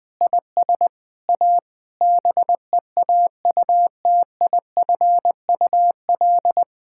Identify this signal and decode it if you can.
Morse code, heard as a keyed tone, about 21 wpm, 700 Hz. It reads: IS A BEAUTIFUL
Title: Sentences from Top 1000 Words - Review Part 6 - 20wpm